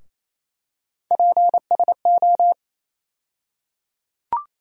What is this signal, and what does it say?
Morse code, heard as a keyed tone, about 28 wpm, 700 Hz. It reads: PSO
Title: USA Call Sign Postfixes - 3 Letters 28wpm